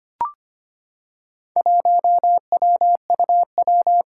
Morse code, heard as a keyed tone, about 25 wpm, 700 Hz. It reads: 1WUW
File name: practice / 25wpm / Four Letter-Number Combinations - 25wpm